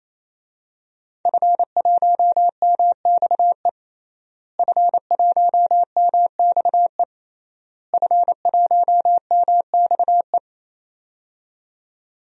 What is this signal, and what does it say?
Morse code, heard as a keyed tone, about 27 wpm, 700 Hz. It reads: F1MXE F1MXE F1MXE
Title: DXCC-100 Call Signs - 5 Characters - Repeated 3x - 28wpm